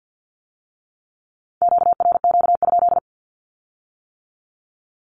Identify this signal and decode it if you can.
Morse code, heard as a keyed tone, about 50 wpm, 700 Hz. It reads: QRQ?
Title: Q-Codes with Spoken Meaning - 50wpm